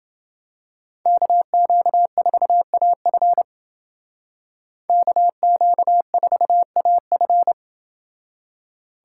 Morse code, heard as a keyed tone, about 29 wpm, 700 Hz. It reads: KQ4AF KQ4AF